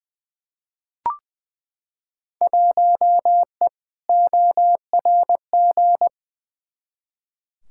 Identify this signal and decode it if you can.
Morse code, heard as a keyed tone, about 20 wpm, 700 Hz. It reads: 1E ORG
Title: Field Day Exchanges - 20wpm